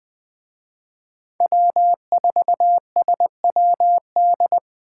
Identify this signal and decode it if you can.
Morse code, heard as a keyed tone, about 21 wpm, 700 Hz. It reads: W4SWD